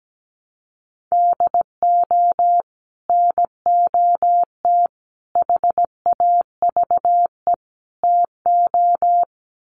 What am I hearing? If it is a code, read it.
Morse code, heard as a keyed tone, about 17 wpm, 700 Hz. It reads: DO NOT HAVE TO
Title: Sets of 4 Words 17wpm